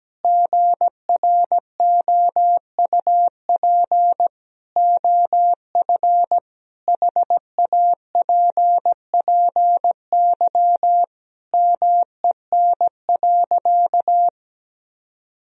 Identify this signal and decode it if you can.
Morse code, heard as a keyed tone, about 17 wpm, 700 Hz. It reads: GROUP OF HAPPY MEN.
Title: Sentences from Top 1000 Words - Review Part 1 - 17wpm